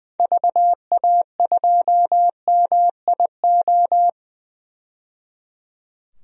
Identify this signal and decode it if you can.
Morse code, heard as a keyed tone, about 20 wpm, 700 Hz. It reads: VA2MIO
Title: DXCC-100 Call Signs - 6 Characters - 20wpm